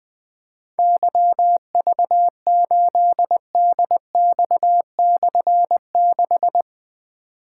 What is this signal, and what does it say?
Morse code, heard as a keyed tone, about 21 wpm, 700 Hz. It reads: YV8DX/6
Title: Difficult Call Signs 20wpm